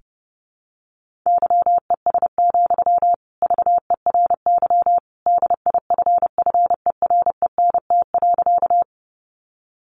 Morse code, heard as a keyed tone, about 30 wpm, 700 Hz. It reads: YES, VERY DIFFERENT.
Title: Sentences from Top 600 Words 30wpm